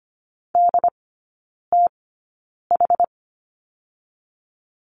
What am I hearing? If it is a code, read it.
Morse code, heard as a keyed tone, about 25 wpm, spaced as for 9 wpm, 690 Hz. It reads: DTH
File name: Lesson 14e - Number 2 - Three Characters 25wpm